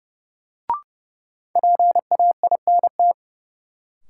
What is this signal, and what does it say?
Morse code, heard as a keyed tone, about 30 wpm, 700 Hz. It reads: PAINT
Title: Sentences from Top 1000 Words - Review Part 6 - New Word Drill - 30wpm